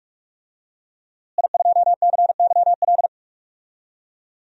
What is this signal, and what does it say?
Morse code, heard as a keyed tone, about 41 wpm, 700 Hz. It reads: I2CYL
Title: DXCC-100 Call Signs - 5 Characters - Repeated 3x - 45wpm